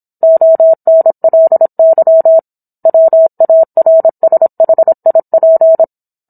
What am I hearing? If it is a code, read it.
Morse code, heard as a keyed tone, about 27 wpm, 650 Hz. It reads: ONLY WARSHIP